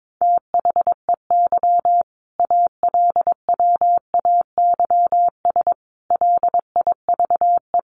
Morse code, heard as a keyed tone, about 22 wpm, 700 Hz. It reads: THEY ALWAYS LIVE
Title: Sentences from Top 1400 Words 22wpm